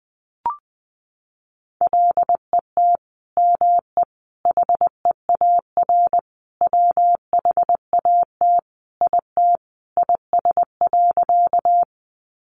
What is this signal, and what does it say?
Morse code, heard as a keyed tone, about 20 wpm, 700 Hz. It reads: LET ME HEAR WHAT IT IS.
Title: Sentences from Top 300 Words 20wpm